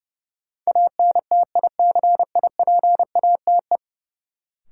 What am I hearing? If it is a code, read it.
Morse code, heard as a keyed tone, about 30 wpm, 700 Hz. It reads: ANTICIPATE
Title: Sentences from Top 1500 Words - New Word Drill - 30wpm